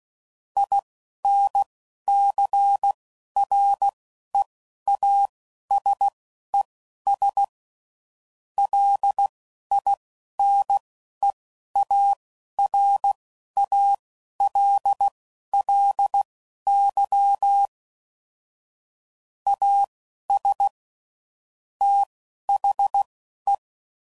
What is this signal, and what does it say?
Morse code, heard as a keyed tone, about 16 wpm, spaced as for 11 wpm, 800 Hz. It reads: INCREASES LINEARALLY AS THE
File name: CW Code Groups @10WPM / Technical Subjects / XL